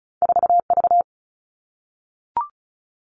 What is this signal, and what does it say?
Morse code, heard as a keyed tone, about 36 wpm, 700 Hz. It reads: 4V
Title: All Cut Numbers - 2-Digit 35wpm